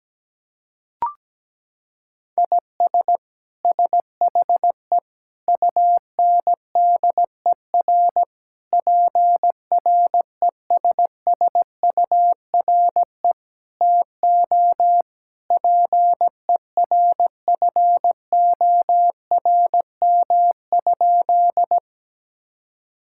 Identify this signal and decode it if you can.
Morse code, heard as a keyed tone, about 17 wpm, 700 Hz. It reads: IS SHE UNDER PRESSURE TO PERFORM?
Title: Sentences from Top 1000 Words - Review Part 6 - 17wpm